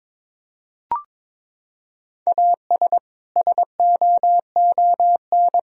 Morse code, heard as a keyed tone, about 22 wpm, 700 Hz. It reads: AS SOON
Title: Sentences from Top 400 Words and Abbreviations 22wpm